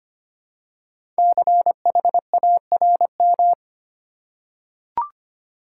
Morse code, heard as a keyed tone, about 25 wpm, 700 Hz. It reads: CHARM
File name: 5 Letter Words - First 3 Letters Given - 25wpm